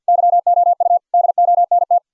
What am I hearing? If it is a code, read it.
Morse code, heard as a keyed tone, about 49 wpm, 690 Hz. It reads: YOU DONT